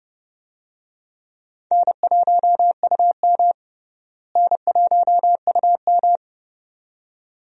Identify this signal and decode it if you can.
Morse code, heard as a keyed tone, about 30 wpm, 700 Hz. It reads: N1UM N1UM
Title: USA Call Signs - 4 Characters - Repeated 2x - 30wpm